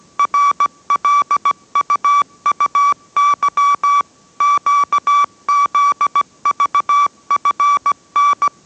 Morse code, heard as a keyed tone, about 17 wpm, 1150 Hz. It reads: RLUUY QZVFN